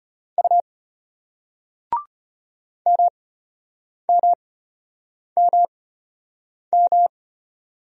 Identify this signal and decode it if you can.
Morse code, heard as a keyed tone, about 33 wpm, spaced as for 8 wpm, 700 Hz. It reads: U MMMM